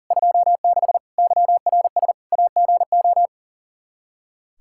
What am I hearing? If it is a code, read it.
Morse code, heard as a keyed tone, about 40 wpm, 700 Hz. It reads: 26 YRS AGO
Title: K1USN SST QSO Practice 40wpm